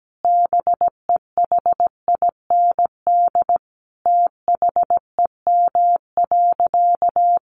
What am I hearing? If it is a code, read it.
Morse code, heard as a keyed tone, about 17 wpm, 700 Hz. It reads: BEHIND THEM.
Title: Sentences from Top 1300 Words 17wpm